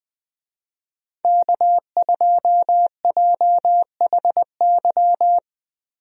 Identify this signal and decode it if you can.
Morse code, heard as a keyed tone, about 20 wpm, 700 Hz. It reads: K2JHY
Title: USA Call Signs - 5 Characters - Repeated 2x - 20wpm